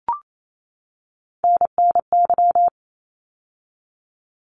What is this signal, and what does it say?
Morse code, heard as a keyed tone, about 28 wpm, 700 Hz. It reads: NNY
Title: ARRL Sections 28wpm